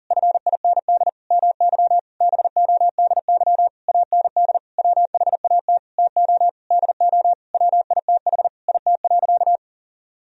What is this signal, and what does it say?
Morse code, heard as a keyed tone, about 37 wpm, 700 Hz. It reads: FIND MY BODY AND WHAT TO DO WITH IT.
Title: Sentences from Top 1000 Words - Review Part 2 - 40wpm